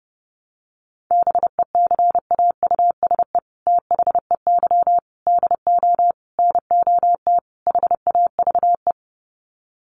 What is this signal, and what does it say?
Morse code, heard as a keyed tone, about 30 wpm, 700 Hz. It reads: BECAUSE THEY DO NOT HAVE
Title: Sets of 5 Words Encore2 30wpm